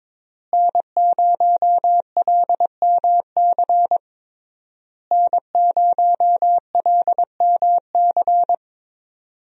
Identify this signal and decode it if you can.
Morse code, heard as a keyed tone, about 22 wpm, 700 Hz. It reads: N0LMC N0LMC
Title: USA Call Signs - 5 Characters - Repeated 3x - 22wpm